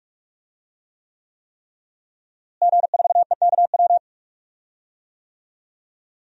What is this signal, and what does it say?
Morse code, heard as a keyed tone, about 46 wpm, 700 Hz. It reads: G4EKW
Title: DXCC-100 Call Signs - 5 Characters - Repeated 3x - 45wpm